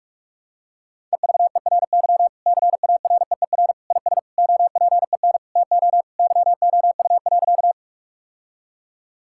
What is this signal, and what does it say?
Morse code, heard as a keyed tone, about 41 wpm, 700 Hz. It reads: EVERY CAREER IS OPEN TO YOU.
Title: Sentences from Top 1000 Words - Review Part 6 - 45wpm